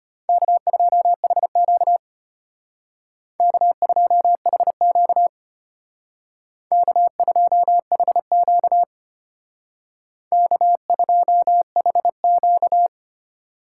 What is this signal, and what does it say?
Morse code, heard as a keyed tone, about 33 wpm, 700 Hz. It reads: K2HQ K2HQ K2HQ K2HQ